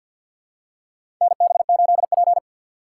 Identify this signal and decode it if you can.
Morse code, heard as a keyed tone, about 51 wpm, 700 Hz. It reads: NB8P
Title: USA Call Signs - 4 Characters - 2x Character Spacing - 50wpm